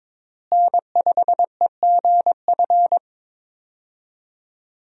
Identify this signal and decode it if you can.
Morse code, heard as a keyed tone, about 22 wpm, 700 Hz. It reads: N5EGF
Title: USA Call Signs - 5 Characters - Repeated 2x - 22wpm